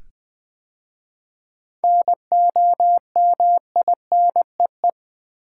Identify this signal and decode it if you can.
Morse code, heard as a keyed tone, about 20 wpm, 690 Hz. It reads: NOMINEE
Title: 7 Letter Words - First 3 Letters Given - 20wpm